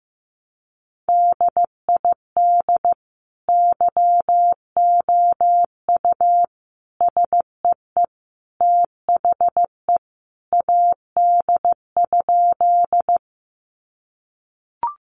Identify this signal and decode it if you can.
Morse code, heard as a keyed tone, about 15 wpm, 700 Hz. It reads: DID YOU SEE THE AD?
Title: Sentences from Top 1300 Words 15wpm